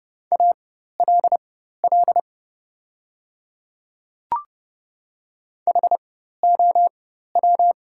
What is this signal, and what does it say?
Morse code, heard as a keyed tone, about 29 wpm, spaced as for 14 wpm, 700 Hz. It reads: ALL HOW
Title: 4x Character Spacing - Top 100 Words - 30wpm